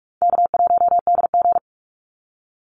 Morse code, heard as a keyed tone, about 44 wpm, 700 Hz. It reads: K1DG
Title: Easy Call Signs Repeated 2x - 45wpm